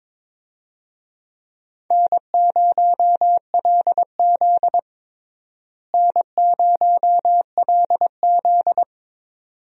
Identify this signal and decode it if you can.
Morse code, heard as a keyed tone, about 22 wpm, 700 Hz. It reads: N0LZ N0LZ
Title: USA Call Signs - 4 Characters - Repeated 2x - 22wpm